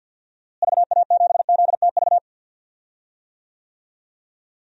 Morse code, heard as a keyed tone, about 50 wpm, 700 Hz. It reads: VA7ZTV
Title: DXCC-100 Call Signs - 6 Characters - Repeated 2x - 50wpm